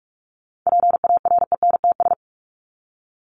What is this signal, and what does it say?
Morse code, heard as a keyed tone, about 44 wpm, 700 Hz. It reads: PARENTS
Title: ES:S Suffix - Top 100 Words with Base Words 45wpm